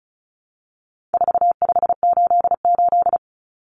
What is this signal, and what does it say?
Morse code, heard as a keyed tone, about 34 wpm, 700 Hz. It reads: 4588
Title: Common Cut Numbers - 4-Digit 35wpm